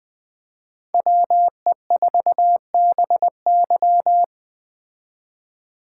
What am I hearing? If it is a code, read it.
Morse code, heard as a keyed tone, about 20 wpm, 700 Hz. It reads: WE4BY
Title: Speed Racing - USA 2x2 Call Signs 17wpm